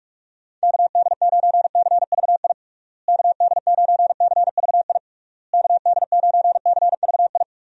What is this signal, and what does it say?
Morse code, heard as a keyed tone, about 46 wpm, 700 Hz. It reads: KD9CVI KD9CVI KD9CVI
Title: USA Call Signs - 6 Characters - Repeated 3x - 45wpm